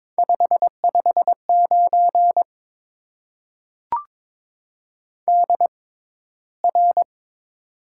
Morse code, heard as a keyed tone, about 22 wpm, spaced as for 13 wpm, 700 Hz. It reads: 559 DR
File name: Top 90 QSO Elements - 6x Character Spacing - 22wpm